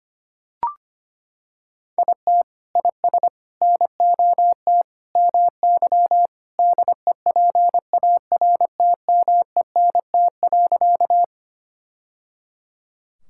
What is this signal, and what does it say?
Morse code, heard as a keyed tone, about 25 wpm, 700 Hz. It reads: IT IS NOT MY DEPARTMENT.
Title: Sentences from Top 1000 Words - Review Part 5 - 25wpm